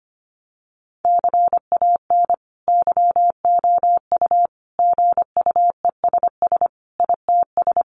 Morse code, heard as a keyed tone, about 25 wpm, 700 Hz. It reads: CAN YOU GUESS ITS
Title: Sentences from Top 1500 Words 25wpm